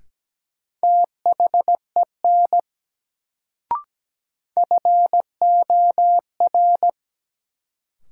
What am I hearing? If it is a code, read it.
Morse code, heard as a keyed tone, about 17 wpm, 700 Hz. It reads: THEN FOR
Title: Sentences from Top 100 Words - New Word Drill - 17wpm